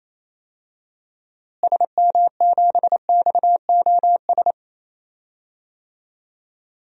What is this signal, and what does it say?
Morse code, heard as a keyed tone, about 28 wpm, 700 Hz. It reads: SM7XOS
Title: DXCC-100 Call Signs - 6 Characters - Repeated 2x - 28wpm